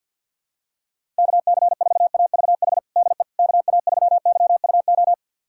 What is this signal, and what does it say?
Morse code, heard as a keyed tone, about 47 wpm, 700 Hz. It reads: KC4AVH DE KA3YUO
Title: USA Callsign DE Callsign - 6 Characters - 50wpm